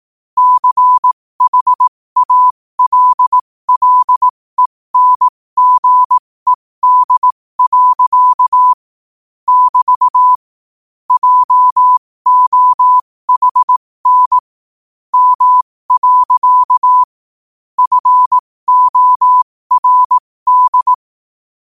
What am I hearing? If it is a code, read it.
Morse code, heard as a keyed tone, about 18 wpm, spaced as for 15 wpm, 1000 Hz. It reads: CHALLENGED. <BT> JOHN M. FORD
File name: qotdfiles / 15wpm